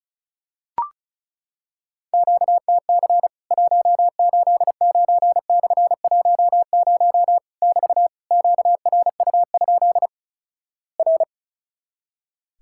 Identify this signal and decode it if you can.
Morse code, heard as a keyed tone, about 35 wpm, 700 Hz. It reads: QTC 189/10 <BT> QRU? R
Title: Worked All Europe DX Contest - QTC Exchange - 35wpm